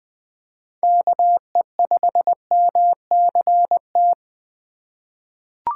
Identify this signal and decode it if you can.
Morse code, heard as a keyed tone, about 20 wpm, 700 Hz. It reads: KE5MCT
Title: USA Call Signs - 6 Characters - 3x Character Spacing - 20wpm